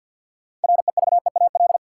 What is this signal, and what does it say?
Morse code, heard as a keyed tone, about 47 wpm, 700 Hz. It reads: REVEAL
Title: Six Letter Words - 50wpm